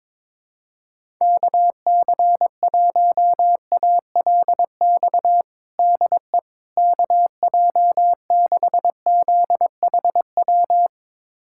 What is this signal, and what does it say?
Morse code, heard as a keyed tone, about 22 wpm, 700 Hz. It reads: KC1ALX DE KJ6ZHW
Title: USA Callsign DE Callsign - 6 Characters - 22wpm